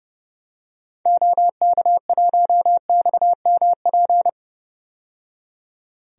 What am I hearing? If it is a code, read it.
Morse code, heard as a keyed tone, about 32 wpm, 700 Hz. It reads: OK1XMP
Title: DXCC-100 Call Signs - 6 Characters - 30wpm